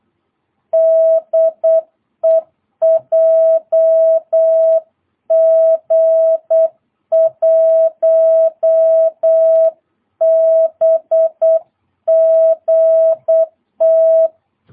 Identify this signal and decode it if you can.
Morse code, heard as a keyed tone, about 8 wpm, 660 Hz. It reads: DE JG1BGT
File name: jh8cbh / 2022cw